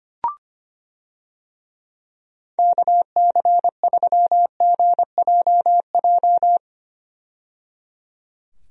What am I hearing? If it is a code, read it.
Morse code, heard as a keyed tone, about 25 wpm, 700 Hz. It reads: KC3GJJ
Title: USA Call Signs - 6 Characters - 25wpm